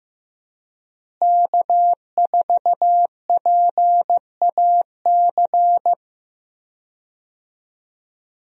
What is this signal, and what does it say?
Morse code, heard as a keyed tone, about 15 wpm, 700 Hz. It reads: K4PAC
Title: USA Call Signs - 5 Characters - Repeated 3x - 15wpm